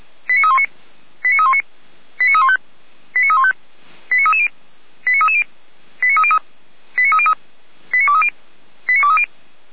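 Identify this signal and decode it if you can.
Five-tone selective call (ZVEI), heard as quick runs of five tones, tones of about 70 ms. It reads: 89319 89319 89315 89315 89330 89330 89303 89303 89230 89230
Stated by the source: ZVEI five-tone selective call received by SDR.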